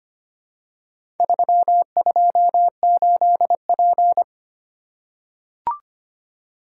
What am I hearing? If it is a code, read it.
Morse code, heard as a keyed tone, about 25 wpm, 700 Hz. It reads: 328P
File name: practice / 25wpm / Four Letter-Number Combinations - 25wpm